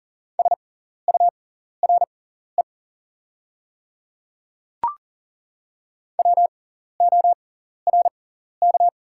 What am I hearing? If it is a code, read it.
Morse code, heard as a keyed tone, about 40 wpm, spaced as for 14 wpm, 700 Hz. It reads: SURE WORK
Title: 6x Character Spacing - Top 300 Words - 40wpm